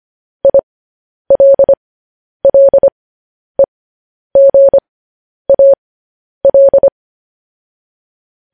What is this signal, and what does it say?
Morse code, heard as a keyed tone, about 25 wpm, spaced as for 10 wpm, 550 Hz. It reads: ILLEGAL